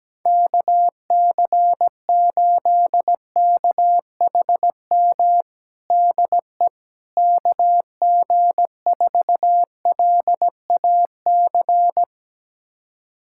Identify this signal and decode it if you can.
Morse code, heard as a keyed tone, about 17 wpm, 700 Hz. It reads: KC8KHM DE KG4LAC
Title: USA Callsign DE Callsign - 6 Characters - 17wpm